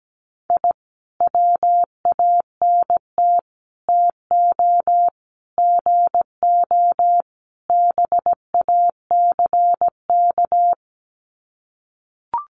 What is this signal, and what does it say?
Morse code, heard as a keyed tone, about 17 wpm, 700 Hz. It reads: I WANT TO GO BACK
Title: Sets of 5 Words Encore 17wpm